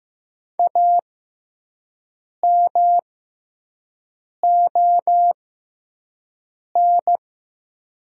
Morse code, heard as a keyed tone, about 15 wpm, spaced as for 5 wpm, 700 Hz. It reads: AMON
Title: 6x Character Spacing - Top 500 Words - 15wpm